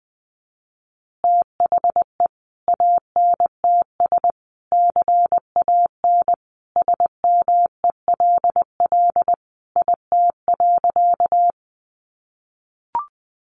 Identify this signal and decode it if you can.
Morse code, heard as a keyed tone, about 20 wpm, 700 Hz. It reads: THE ANTS CAN SMELL IT.
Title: Additional Easy Phrases 20wpm